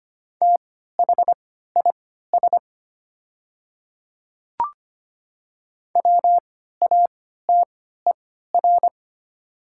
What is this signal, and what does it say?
Morse code, heard as a keyed tone, about 25 wpm, spaced as for 14 wpm, 700 Hz. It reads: THIS WATER
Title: 3x Character Spacing - Top 100 Words - 25wpm